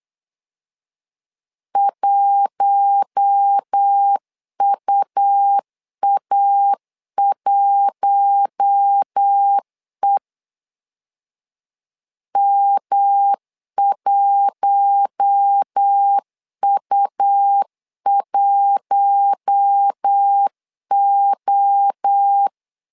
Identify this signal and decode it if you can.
Morse code, heard as a keyed tone, about 8 wpm, 800 Hz. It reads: 1UA1E M1U1O
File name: LEZIONI / 4 Q1 8wpm 3035